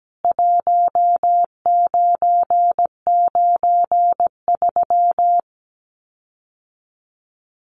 Morse code, heard as a keyed tone, about 17 wpm, 700 Hz. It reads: 1993
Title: CWT Exchange 17wpm